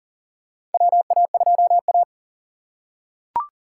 Morse code, heard as a keyed tone, about 40 wpm, 700 Hz. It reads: WA2A
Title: USA Call Signs - 3x Character Spacing - 40wpm